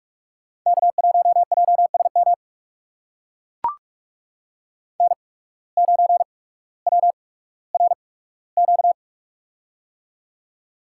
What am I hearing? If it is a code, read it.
Morse code, heard as a keyed tone, about 45 wpm, spaced as for 19 wpm, 700 Hz. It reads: K1JSM N9WRQ
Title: USA Call Signs - 5 Characters - 8x Character Spacing - 45wpm